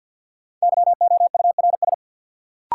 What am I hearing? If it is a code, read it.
Morse code, heard as a keyed tone, about 53 wpm, 700 Hz. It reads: YOURS
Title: Sentences from Top 1500 Words - New Word Drill - 50wpm